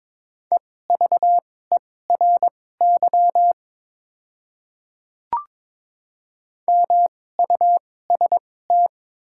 Morse code, heard as a keyed tone, about 21 wpm, spaced as for 16 wpm, 700 Hz. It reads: EVERY MUST
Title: 2x Character Spacing - Top 300 Words - 22wpm